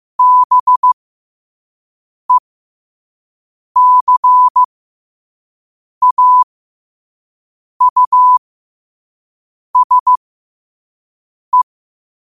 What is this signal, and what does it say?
Morse code, heard as a keyed tone, about 15 wpm, spaced as for 5 wpm, 1000 Hz. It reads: BECAUSE
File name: qotdfiles / 5wpm